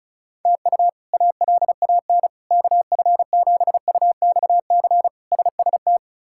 Morse code, heard as a keyed tone, about 36 wpm, 700 Hz. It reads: TU ALAN KF7UXC SST
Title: K1USN SST QSO Practice - 35wpm